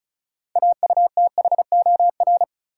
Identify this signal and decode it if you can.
Morse code, heard as a keyed tone, about 36 wpm, 700 Hz. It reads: AUTHOR
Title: Sentences from Top 1200 Words - New Word Drill - 35wpm